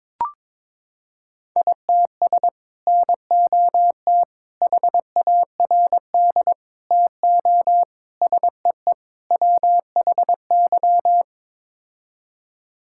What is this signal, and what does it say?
Morse code, heard as a keyed tone, about 22 wpm, 700 Hz. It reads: ITS NOT HARD TO SEE WHY
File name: practice / 22wpm / Sets of 6 Words - 22wpm